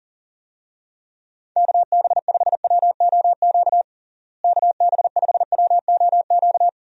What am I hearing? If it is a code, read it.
Morse code, heard as a keyed tone, about 40 wpm, 700 Hz. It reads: KB5WOQ KB5WOQ
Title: USA Call Signs - 6 Characters - Repeated 3x - 40wpm